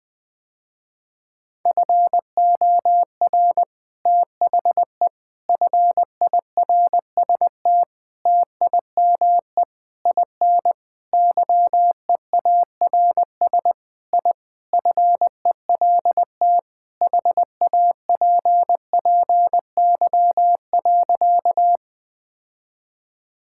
Morse code, heard as a keyed tone, about 20 wpm, 700 Hz. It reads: FOR THE FIRST TIME IN YEARS I FELT HAPPY.
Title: Sentences with 5-Word Phrases - Spoken Beginning - 20wpm